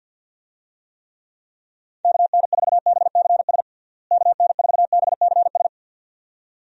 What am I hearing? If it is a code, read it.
Morse code, heard as a keyed tone, about 49 wpm, 700 Hz. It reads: KN4BCS KN4BCS
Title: USA Call Signs - 6 Characters - Repeated 2x - 50wpm